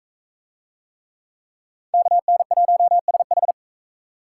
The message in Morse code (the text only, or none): KN1SH